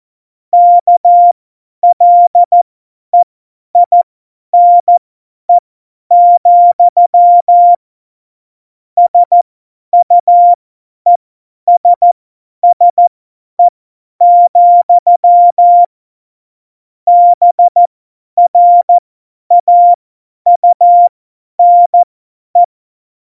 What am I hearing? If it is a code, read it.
Morse code, heard as a keyed tone, about 14 wpm, spaced as for 10 wpm, 700 Hz. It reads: KLEINE, SUESSE, BRAUNE
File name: Morsekurs / Witze / Witz 22. Januar 2019, 50 F70